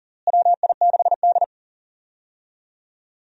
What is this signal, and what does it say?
Morse code, heard as a keyed tone, about 40 wpm, 700 Hz. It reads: WI6D
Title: USA Call Signs - 4 Characters - Repeated 3x - 40wpm